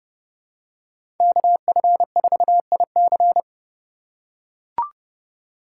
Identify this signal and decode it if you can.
Morse code, heard as a keyed tone, about 31 wpm, 700 Hz. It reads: KF4IC
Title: USA Call Signs - 5 Characters - 4x Character Spacing - 30wpm